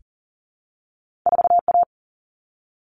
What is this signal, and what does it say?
Morse code, heard as a keyed tone, about 38 wpm, 700 Hz. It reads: 4A
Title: All Cut Numbers - 2-Digit 40wpm